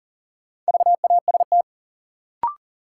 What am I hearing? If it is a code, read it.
Morse code, heard as a keyed tone, about 40 wpm, 700 Hz. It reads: VAST
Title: Sentences from Top 1300 Words - New Word Drill Alt - 40wpm